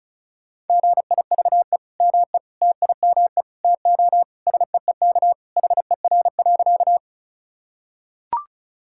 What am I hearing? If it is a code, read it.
Morse code, heard as a keyed tone, about 34 wpm, 700 Hz. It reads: GIVE ME TIME TO SEEK HER.
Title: Sentences from Top 1600 Words 35wpm